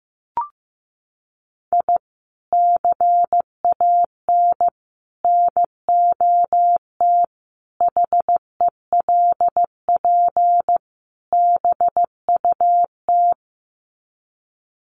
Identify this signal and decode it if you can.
Morse code, heard as a keyed tone, about 15 wpm, 700 Hz. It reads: I CAN NOT HELP BUT
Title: Sets of 5 Words Encore2 15wpm